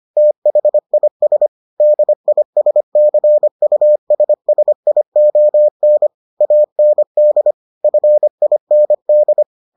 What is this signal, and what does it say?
Morse code, heard as a keyed tone, about 25 wpm, 600 Hz. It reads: THIS DISCUSSION AND FIND